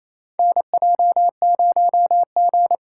Morse code, heard as a keyed tone, about 28 wpm, 700 Hz. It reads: NJ0G
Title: USA Call Signs - 2x Character Spacing - 28wpm